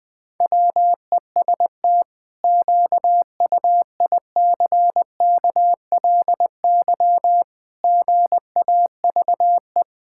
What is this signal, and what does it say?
Morse code, heard as a keyed tone, about 20 wpm, 700 Hz. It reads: WEST QUICKLY GAVE
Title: Pangram Challenge v2 - 20wpm